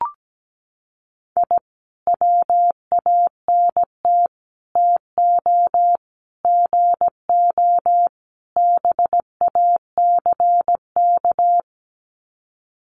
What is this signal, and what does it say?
Morse code, heard as a keyed tone, about 17 wpm, 700 Hz. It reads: I WANT TO GO BACK
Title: Sets of 5 Words Encore 17wpm